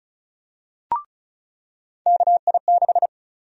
Morse code, heard as a keyed tone, about 36 wpm, 700 Hz. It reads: KI6
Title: USA Call Sign Prefixes - 35wpm